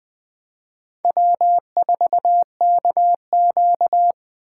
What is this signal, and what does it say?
Morse code, heard as a keyed tone, about 20 wpm, 700 Hz. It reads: W4KQ
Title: Speed Racing - USA 1x2 Call Signs 17wpm